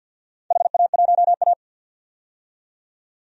Morse code, heard as a keyed tone, about 50 wpm, 700 Hz. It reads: HA1A
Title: DXCC-100 Call Signs - 4 Characters - 50wpm